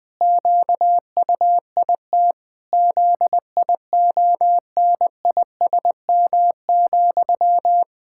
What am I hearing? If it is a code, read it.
Morse code, heard as a keyed tone, about 20 wpm, 700 Hz. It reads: QUIT ZIONISM,
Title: Pangram Challenge v2 - 20wpm